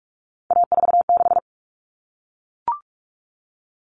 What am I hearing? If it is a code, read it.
Morse code, heard as a keyed tone, about 44 wpm, 700 Hz. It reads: A46
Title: Common Cut Numbers - 3-Digit 45wpm